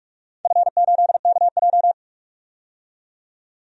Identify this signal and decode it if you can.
Morse code, heard as a keyed tone, about 44 wpm, 690 Hz. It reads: F8KJ
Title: DXCC-100 Call Signs - 4 Characters - 45wpm